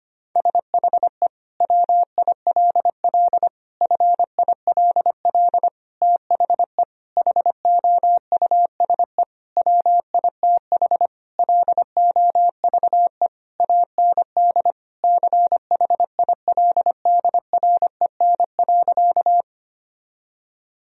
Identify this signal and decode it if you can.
Morse code, heard as a keyed tone, about 25 wpm, 700 Hz. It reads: SHE WILL FILL THE HOUSE WITH LOVE AND CHILDREN.
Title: Sentences from Top 1000 Words - Review Part 3 - 25wpm